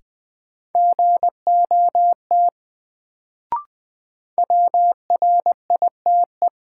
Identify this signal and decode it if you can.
Morse code, heard as a keyed tone, about 20 wpm, 700 Hz. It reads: GOT WRITE
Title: Sentences from Top 300 Words - New Word Drill - 20wpm